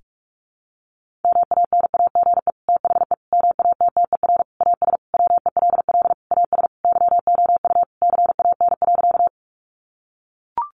Morse code, heard as a keyed tone, about 44 wpm, 700 Hz. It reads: MANAGE THE MATTER AS WELL AS YOU CAN.
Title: Sentences from Top 1400 Words 45wpm